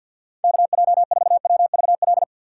Morse code, heard as a keyed tone, about 51 wpm, 700 Hz. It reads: KJ4WVL
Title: USA Call Signs - 6 Characters - 6x Character Spacing - 50wpm